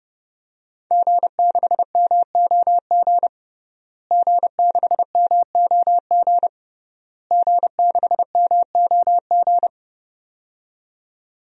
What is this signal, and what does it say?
Morse code, heard as a keyed tone, about 31 wpm, 700 Hz. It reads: G6MOG G6MOG G6MOG
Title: DXCC-100 Call Signs - 5 Characters - Repeated 3x - 30wpm